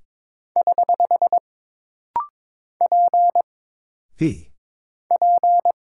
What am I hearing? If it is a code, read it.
Morse code, heard as a keyed tone, about 22 wpm, 700 Hz. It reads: <HH> P P
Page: This rapid-fire timing is geared to push your ICR (Instant Character Recognition) skills.